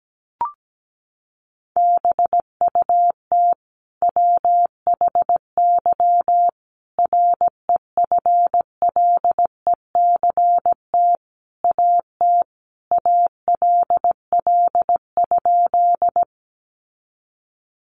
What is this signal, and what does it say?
Morse code, heard as a keyed tone, about 17 wpm, 700 Hz. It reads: BUT WHY REFLECT AT ALL?
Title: Sentences from Top 1500 Words 17wpm